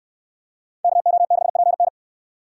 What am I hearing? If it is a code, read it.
Morse code, heard as a keyed tone, about 64 wpm, 700 Hz. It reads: KO6PN